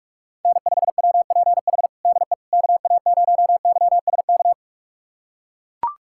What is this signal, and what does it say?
Morse code, heard as a keyed tone, about 42 wpm, 700 Hz. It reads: N5WPH DE KA0YSK
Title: USA Callsign DE Callsign 45wpm